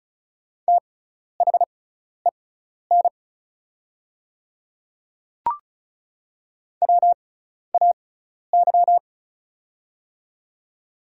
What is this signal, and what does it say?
Morse code, heard as a keyed tone, about 34 wpm, spaced as for 12 wpm, 700 Hz. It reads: THEN WAY
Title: 6x Character Spacing - Top 100 Words - 35wpm